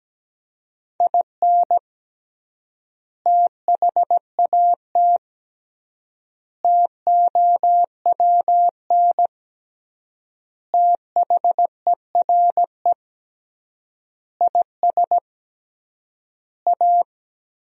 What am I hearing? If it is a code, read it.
Morse code, heard as a keyed tone, about 17 wpm, 700 Hz. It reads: IN THAT TOWN THERE IS A
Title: 3x Word-Spacing - Sentences from Top 500 Words - 17wpm